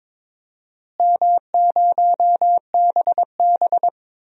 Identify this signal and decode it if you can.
Morse code, heard as a keyed tone, about 22 wpm, 700 Hz. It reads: M0BB
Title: DXCC-100 Call Signs - 4 Characters - Repeated 3x - 22wpm